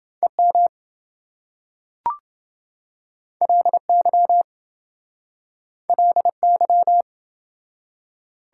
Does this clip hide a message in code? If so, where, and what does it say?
Yes: Morse code from 0.2 to 7.0 s, heard as a keyed tone, about 27 wpm, 700 Hz. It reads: EM LY LY